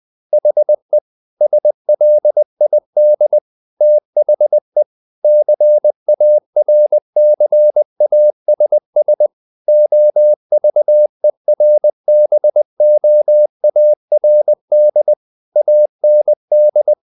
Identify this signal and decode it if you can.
Morse code, heard as a keyed tone, about 20 wpm, 600 Hz. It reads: HE SLID THE CARCASS OVERBOARD AND
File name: the old man and the sea part 2 20wpm